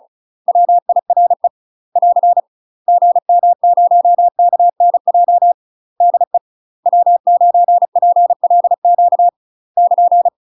Morse code, heard as a keyed tone, about 34 wpm, 700 Hz. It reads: WIRE <AR> GM0KNJ DE W9PLQ <KN>
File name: UK MASTER QSOs TRAINING / 35WPM / UK MASTER QSOs 0037